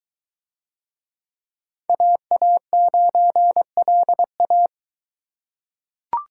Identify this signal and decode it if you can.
Morse code, heard as a keyed tone, about 21 wpm, 700 Hz. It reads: AA9LA